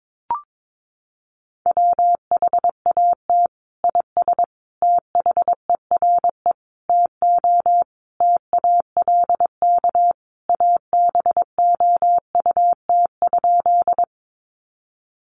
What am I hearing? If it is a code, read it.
Morse code, heard as a keyed tone, about 22 wpm, 700 Hz. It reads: WHAT IS THERE TO TALK ABOUT?
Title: Sentences from Top 400 Words 22wpm